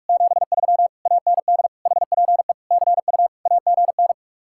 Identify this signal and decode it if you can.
Morse code, heard as a keyed tone, about 59 wpm, spaced as for 52 wpm, 700 Hz. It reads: 73 AND HPE CU AGN
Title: Speed-Racing Common QSO Phrases Encore - 45wpm